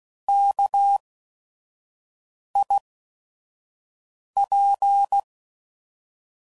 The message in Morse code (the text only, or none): KIP